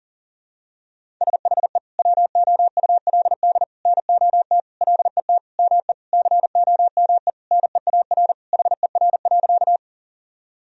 Morse code, heard as a keyed tone, about 40 wpm, 700 Hz. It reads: SHE WOULD NOT LET ME COME NEAR HER.
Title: Sentences from Top 1000 Words - Review Part 1 - 40wpm